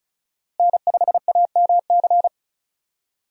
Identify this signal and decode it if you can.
Morse code, heard as a keyed tone, about 36 wpm, 700 Hz. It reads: N5AMC
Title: USA Call Signs - 5 Characters - Repeated 2x - 35wpm